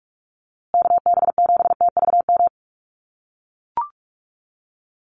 Morse code, heard as a keyed tone, about 44 wpm, 690 Hz. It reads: KB7TVM
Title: USA Call Signs Repeated 3x 45wpm